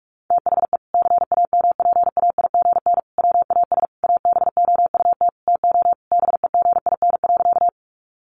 Morse code, heard as a keyed tone, about 44 wpm, 700 Hz. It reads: THE CAMPAIGN WAS ABOUT TO BEGIN.
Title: Sentences from Top 1200 Words 45wpm